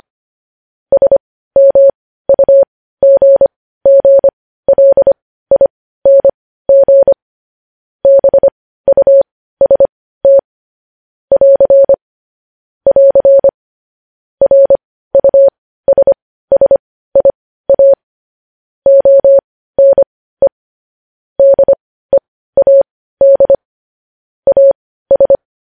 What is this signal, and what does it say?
Morse code, heard as a keyed tone, about 25 wpm, spaced as for 15 wpm, 550 Hz. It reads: SMUGGLING BUST <AR> <AR> RUSSIA ONE DEAD AS